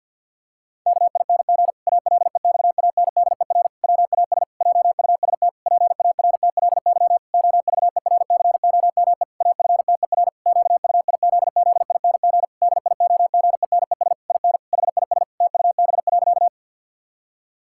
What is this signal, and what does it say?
Morse code, heard as a keyed tone, about 52 wpm, 700 Hz. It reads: KING ALEXANDER WAS JUST PARTLY OVERCOME AFTER QUIZZING DIOGENES IN HIS TUB.
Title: Pangram Challenge v2 - 50wpm